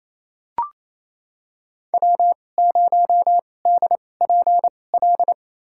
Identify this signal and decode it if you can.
Morse code, heard as a keyed tone, about 28 wpm, spaced as for 20 wpm, 700 Hz. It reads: W0DPL
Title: USA Call Signs - 5 Characters - 2x Character Spacing - 28wpm